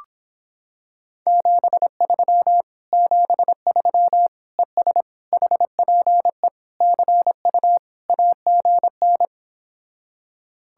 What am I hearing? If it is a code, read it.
Morse code, heard as a keyed tone, about 26 wpm, 700 Hz. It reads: 73 73 ES HPE CU AGN